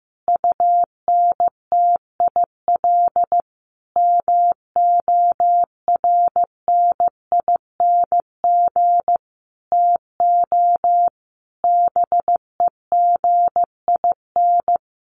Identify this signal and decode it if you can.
Morse code, heard as a keyed tone, about 15 wpm, 700 Hz. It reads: UNTIL MORNING TO BEGIN
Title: Sentences from Top 1300 Words 15wpm